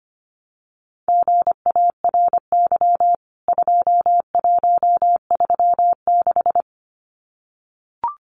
Morse code, heard as a keyed tone, about 25 wpm, 700 Hz. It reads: GARY 2136
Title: CWT Exchange 25wpm